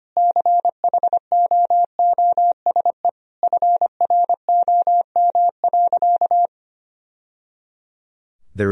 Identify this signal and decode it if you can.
Morse code, heard as a keyed tone, about 25 wpm, 700 Hz. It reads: CHOOSE FROM.
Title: Sentences from Top 1000 Words - Review Part 4 - 25wpm